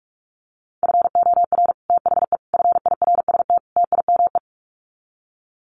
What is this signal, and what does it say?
Morse code, heard as a keyed tone, about 44 wpm, 700 Hz. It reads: FOR THE FIRST TIME
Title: Sets of 4 Words 45wpm